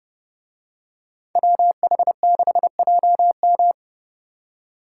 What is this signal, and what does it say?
Morse code, heard as a keyed tone, about 29 wpm, 700 Hz. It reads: WH6JM